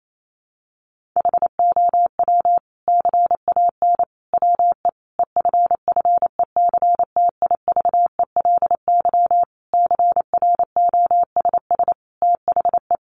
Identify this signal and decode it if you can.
Morse code, heard as a keyed tone, about 28 wpm, 700 Hz. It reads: HOW CAN WE EFFECTIVELY CROSS THE
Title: Sentences from Top 1400 Words 28wpm